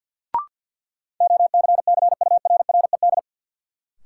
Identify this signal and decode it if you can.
Morse code, heard as a keyed tone, about 50 wpm, 700 Hz. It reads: OCCURRED
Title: Sets of 5 Words Encore - New Word Drill - 50wpm